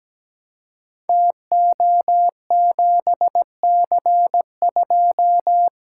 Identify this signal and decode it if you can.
Morse code, heard as a keyed tone, about 17 wpm, 700 Hz. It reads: TO7C2
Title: Code Groups - 5 Letter:Numbers - 17wpm